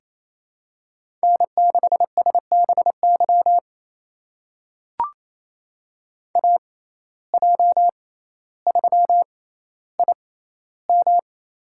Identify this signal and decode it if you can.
Morse code, heard as a keyed tone, about 28 wpm, spaced as for 14 wpm, 700 Hz. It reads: N6SBY AJ3IM
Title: USA Call Signs - 5 Characters - 6x Character Spacing - 28wpm